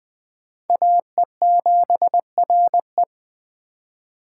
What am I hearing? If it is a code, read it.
Morse code, heard as a keyed tone, about 21 wpm, 700 Hz. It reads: AE7RE